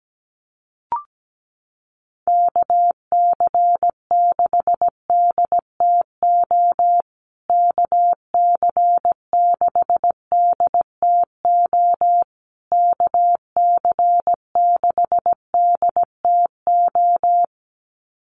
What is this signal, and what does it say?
Morse code, heard as a keyed tone, about 17 wpm, 700 Hz. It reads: KC6DTO KC6DTO KC6DTO
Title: USA Call Signs Repeated 3x 17wpm